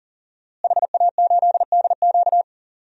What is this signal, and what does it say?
Morse code, heard as a keyed tone, about 40 wpm, 700 Hz. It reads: HA8DQ
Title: DXCC-100 Call Signs - 5 Characters - Repeated 3x - 40wpm